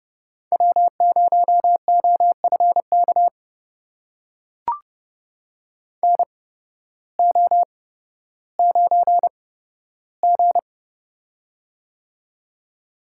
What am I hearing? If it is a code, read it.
Morse code, heard as a keyed tone, about 31 wpm, spaced as for 14 wpm, 700 Hz. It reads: W0OFK NO9G
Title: USA Call Signs - 8x Character Spacing - 30wpm